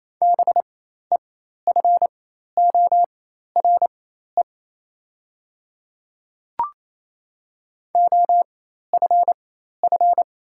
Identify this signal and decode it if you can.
Morse code, heard as a keyed tone, about 27 wpm, spaced as for 13 wpm, 690 Hz. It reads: BEFORE OFF
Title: 4x Character Spacing - Top 200 Words - 28wpm